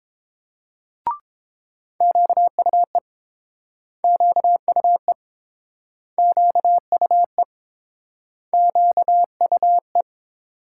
Morse code, heard as a keyed tone, about 29 wpm, 700 Hz. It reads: QUE QUE QUE QUE